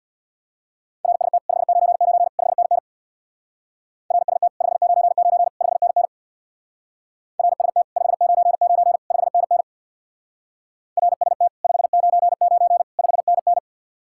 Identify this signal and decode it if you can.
Morse code, heard as a keyed tone, about 62 wpm, 700 Hz. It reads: RST 599 5NN RST 599 5NN RST 599 5NN RST 599 5NN
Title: Speed-Racing Common QSO Phrases - 50wpm